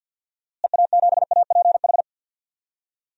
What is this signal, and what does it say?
Morse code, heard as a keyed tone, about 52 wpm, 700 Hz. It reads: EA7APH
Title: DXCC-100 Call Signs - 6 Characters - Repeated 3x - 50wpm